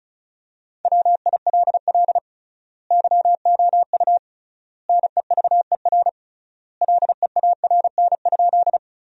Morse code, heard as a keyed tone, about 36 wpm, 700 Hz. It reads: WILL YOU NEVER LEARN?
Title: 3x Word-Spacing - Sentences from Top 300 Words - 35wpm